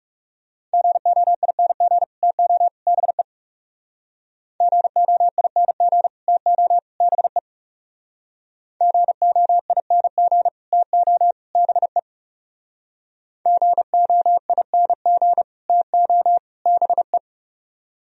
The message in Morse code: GOING TO BE GOING TO BE GOING TO BE GOING TO BE

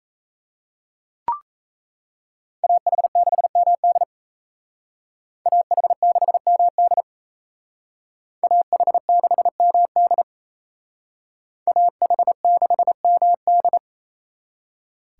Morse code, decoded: AH6MD AH6MD AH6MD AH6MD

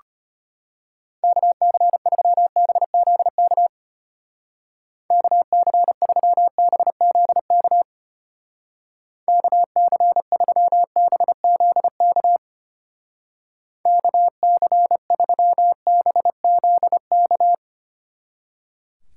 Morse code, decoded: KC3BZK KC3BZK KC3BZK KC3BZK

